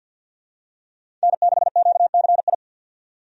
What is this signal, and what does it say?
Morse code, heard as a keyed tone, about 48 wpm, 700 Hz. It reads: N6QCI
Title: USA Call Signs - 5 Characters - 50wpm